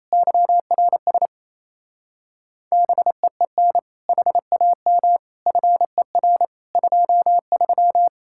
Morse code, heard as a keyed tone, about 37 wpm, 700 Hz. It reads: YRS BEEN HAM FER 23